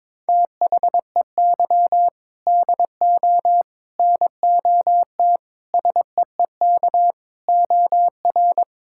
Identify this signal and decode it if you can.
Morse code, heard as a keyed tone, about 22 wpm, 700 Hz. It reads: THEY DO NOT SEEK OR
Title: Sentences from Top 1600 Words 22wpm